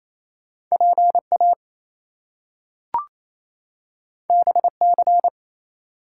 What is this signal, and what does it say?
Morse code, heard as a keyed tone, about 28 wpm, 700 Hz. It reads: PA BC
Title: US States and Canadian Provinces 28wpm